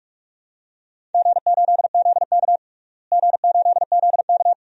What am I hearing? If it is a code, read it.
Morse code, heard as a keyed tone, about 46 wpm, 700 Hz. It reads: G8ZK G8ZK
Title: DXCC-100 Call Signs - 4 Characters - Repeated 2x - 45wpm